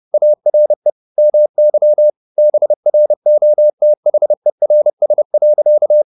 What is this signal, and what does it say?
Morse code, heard as a keyed tone, about 30 wpm, 600 Hz. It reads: ARE MY BROTHERS.